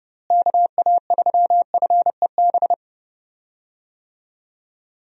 Morse code, heard as a keyed tone, about 29 wpm, 700 Hz. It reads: KA3FEB
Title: USA Call Signs - 6 Characters - Repeated 3x - 30wpm